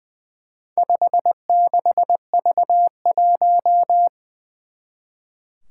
Morse code, heard as a keyed tone, about 20 wpm, 700 Hz. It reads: 56V1
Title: Four Challenging Letter-Number Combinations 20wpm